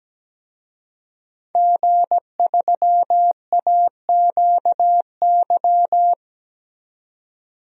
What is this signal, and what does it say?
Morse code, heard as a keyed tone, about 17 wpm, 700 Hz. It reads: G3AQY
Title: DXCC-100 Call Signs - 5 Characters - 17wpm